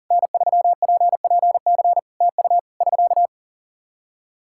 Morse code, heard as a keyed tone, about 37 wpm, 700 Hz. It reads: N3PPC TU <SK>
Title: K1USN SST QSO Practice 40wpm